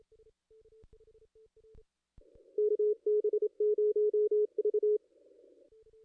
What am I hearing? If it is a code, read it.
Morse code, heard as a keyed tone, about 27 wpm, 420 Hz. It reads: KB0V
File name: contests / ss-cw-k5tr-op-n5rz -2200